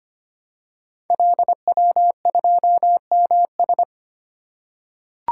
Morse code, heard as a keyed tone, about 25 wpm, 700 Hz. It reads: LW2MS
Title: Code Groups - 5 Letter:Numbers - 25wpm